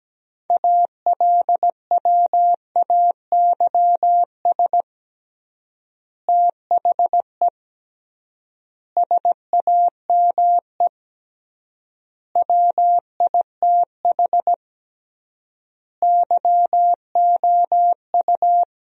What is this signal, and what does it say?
Morse code, heard as a keyed tone, about 17 wpm, 700 Hz. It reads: ALWAYS THE SAME WITH YOU
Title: 3x Word-Spacing - Sentences from Top 300 Words - 17wpm